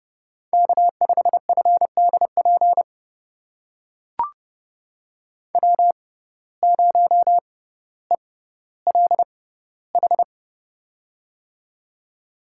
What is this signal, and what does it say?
Morse code, heard as a keyed tone, about 30 wpm, spaced as for 15 wpm, 700 Hz. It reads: K5FDP W0ELH
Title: USA Call Signs - 6x Character Spacing - 30wpm